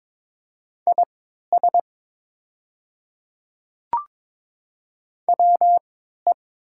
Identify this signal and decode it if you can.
Morse code, heard as a keyed tone, about 22 wpm, spaced as for 13 wpm, 700 Hz. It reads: IS WE
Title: 3x Character Spacing - Top 100 Words - 22wpm